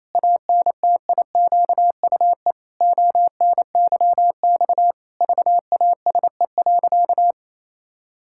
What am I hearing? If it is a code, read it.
Morse code, heard as a keyed tone, about 28 wpm, 700 Hz. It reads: ANTIQUE ONYX VASE.
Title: Pangram Challenge v2 - 28wpm